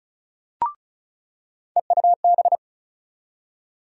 Morse code, heard as a keyed tone, about 36 wpm, 700 Hz. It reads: EUB
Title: USA Call Sign Postfixes - 3 Letters 35wpm